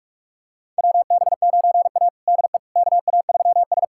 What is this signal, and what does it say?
Morse code, heard as a keyed tone, about 41 wpm, 700 Hz. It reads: WB9A DE KA3S
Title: USA Callsign DE Callsign - 4 Characters - 45wpm